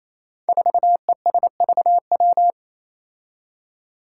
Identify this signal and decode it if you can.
Morse code, heard as a keyed tone, about 28 wpm, 700 Hz. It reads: 4ESVW
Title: Five Challenging Letter-Number Combinations 28wpm